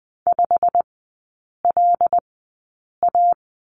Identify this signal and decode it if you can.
Morse code, heard as a keyed tone, about 20 wpm, spaced as for 8 wpm, 700 Hz. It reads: 5LA
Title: 166 Lesson 27e - Number 7 - Three Characters 20wpm